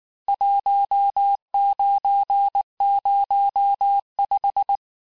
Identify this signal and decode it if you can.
Morse code, heard as a keyed tone, about 19 wpm, 800 Hz. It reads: 1905
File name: cours-du-lundi files / cours-1589-texte-clair-vitesse-18mpm-puis-23mpm